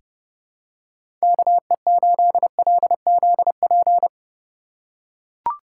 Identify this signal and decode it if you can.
Morse code, heard as a keyed tone, about 30 wpm, 700 Hz. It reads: KE8LZP
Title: USA Call Signs - 6 Characters - 2x Character Spacing - 30wpm